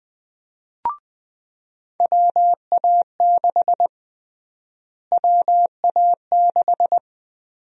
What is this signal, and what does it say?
Morse code, heard as a keyed tone, about 20 wpm, 700 Hz. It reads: WA6 WA6
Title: USA Call Sign Prefixes - Repeated 2x - 20wpm